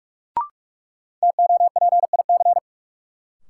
Morse code, heard as a keyed tone, about 44 wpm, 700 Hz. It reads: TOPIC
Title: Sentences from Top 1200 Words - New Word Drill - 45wpm